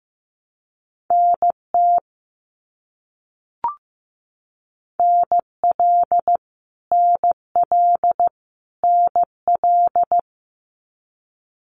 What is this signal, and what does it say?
Morse code, heard as a keyed tone, about 15 wpm, 700 Hz. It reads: NT NL NL NL
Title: Canadian Provinces and Territories - Repeated 3x - 15wpm